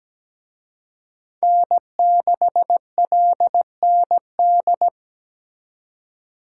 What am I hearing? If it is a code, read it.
Morse code, heard as a keyed tone, about 20 wpm, 700 Hz. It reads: N6LND
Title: Speed Racing - USA 1x3 Call Signs 17wpm